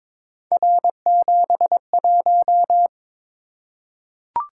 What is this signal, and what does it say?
Morse code, heard as a keyed tone, about 22 wpm, 700 Hz. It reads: R71
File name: practice / 22wpm / Three Letter-Number Combinations - 22wpm